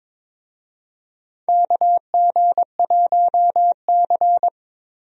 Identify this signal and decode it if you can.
Morse code, heard as a keyed tone, about 22 wpm, 700 Hz. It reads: KG1C
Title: USA Call Signs - 4 Characters - Repeated 3x - 22wpm